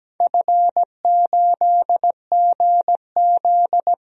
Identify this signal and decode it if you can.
Morse code, heard as a keyed tone, about 17 wpm, 700 Hz. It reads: F8GZ
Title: DXCC-100 Call Signs - 4 Characters - Repeated 2x - 17wpm